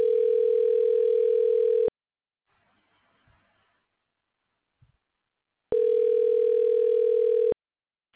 Ringback tone (North America), 440 + 480 Hz.